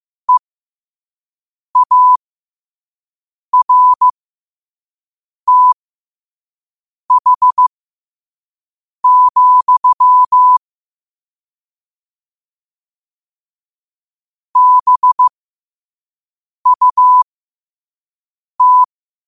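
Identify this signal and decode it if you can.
Morse code, heard as a keyed tone, about 15 wpm, spaced as for 5 wpm, 1000 Hz. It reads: EARTH, BUT
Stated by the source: Morse code quote of the day at 5 wpm.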